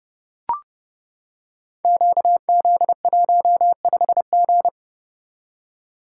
Morse code, heard as a keyed tone, about 30 wpm, 700 Hz. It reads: QZ15G
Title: Code Groups - 5 Letter:Numbers - 30wpm